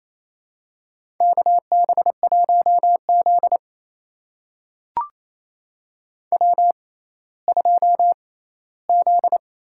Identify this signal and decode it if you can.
Morse code, heard as a keyed tone, about 27 wpm, spaced as for 16 wpm, 700 Hz. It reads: KB1Z W2Z
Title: USA Call Signs - 4 Characters - 6x Character Spacing - 28wpm